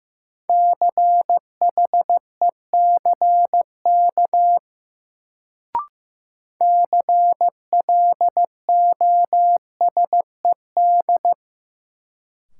Morse code, heard as a keyed tone, about 15 wpm, 700 Hz. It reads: CHECK CLOSED
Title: Sentences from Top 1400 Words - New Word Drill Alt - 15wpm